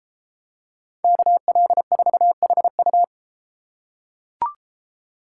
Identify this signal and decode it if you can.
Morse code, heard as a keyed tone, about 32 wpm, 700 Hz. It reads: KL4HU